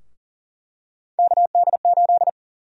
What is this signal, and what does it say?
Morse code, heard as a keyed tone, about 40 wpm, 700 Hz. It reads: KD8
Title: USA Call Sign Prefixes - Repeated 2x - 40wpm